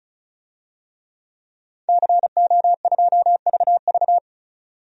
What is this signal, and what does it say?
Morse code, heard as a keyed tone, about 36 wpm, 700 Hz. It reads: CO2VV
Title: DXCC-100 Call Signs - 5 Characters - 35wpm